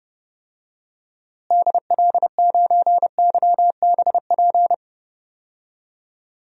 Morse code, heard as a keyed tone, about 30 wpm, 700 Hz. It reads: DL9YBP
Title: DXCC-100 Call Signs - 6 Characters - Repeated 2x - 30wpm